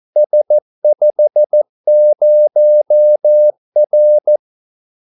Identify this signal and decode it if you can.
Morse code, heard as a keyed tone, about 14 wpm, 600 Hz. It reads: S50R
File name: Callsigns-14wpm